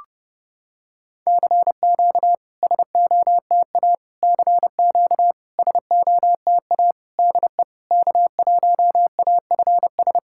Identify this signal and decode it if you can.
Morse code, heard as a keyed tone, about 30 wpm, 690 Hz. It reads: CQ SOTA CQ SOTA DE K1AFS
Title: SOTA QSO Practice 30wpm